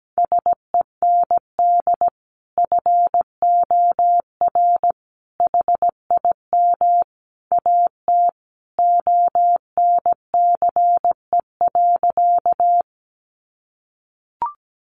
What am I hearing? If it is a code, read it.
Morse code, heard as a keyed tone, about 17 wpm, 700 Hz. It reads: SEND FOR HIM AT ONCE.
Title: Sentences from Top 1500 Words 17wpm